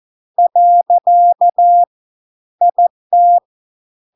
Morse code, heard as a keyed tone, about 14 wpm, 700 Hz. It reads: . IT